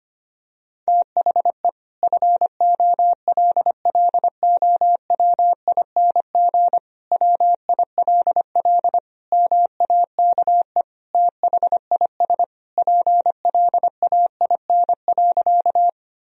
Morse code, heard as a keyed tone, about 25 wpm, 700 Hz. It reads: THE FOLLOWING WILL MAKE THIS PLAIN.
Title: Sentences from Top 1000 Words - Review Part 4 - 25wpm